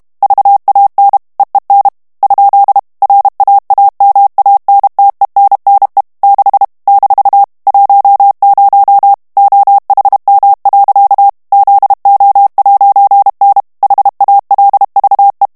Morse code, heard as a keyed tone, about 32 wpm, 800 Hz. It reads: VAN EEN ? RAAMANTENNE 6 - 10 OHM. ZO'N HALVE
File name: morse mp3 / HOLLANDAIS texte clair 1 vitesse 30 mots minute